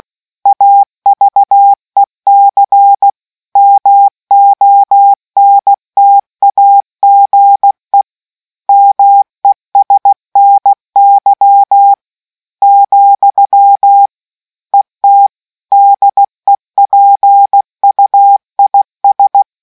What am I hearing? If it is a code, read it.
Morse code, heard as a keyed tone, about 16 wpm, 800 Hz. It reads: AVEC MONTAGE MESNY , ET DEPUIS